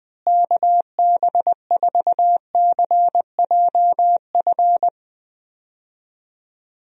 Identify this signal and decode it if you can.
Morse code, heard as a keyed tone, about 20 wpm, 700 Hz. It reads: KB4CJF
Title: USA Call Signs - 6 Characters - Repeated 2x - 20wpm